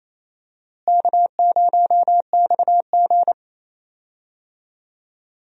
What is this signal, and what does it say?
Morse code, heard as a keyed tone, about 28 wpm, 700 Hz. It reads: K0XG
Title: USA Call Signs - 4 Characters - Repeated 3x - 28wpm